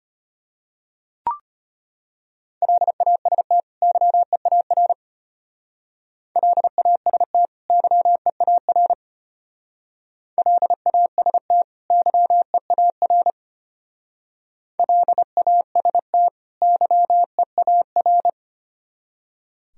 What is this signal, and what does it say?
Morse code, heard as a keyed tone, about 30 wpm, 700 Hz. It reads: LAST YEAR LAST YEAR LAST YEAR LAST YEAR
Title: Speed-Racing - Sets of 2 Words 25wpm